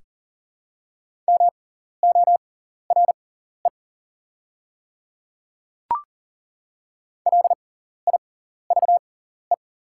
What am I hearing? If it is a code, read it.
Morse code, heard as a keyed tone, about 39 wpm, spaced as for 14 wpm, 690 Hz. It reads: MORE LIVE
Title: 6x Character Spacing - Top 300 Words - 40wpm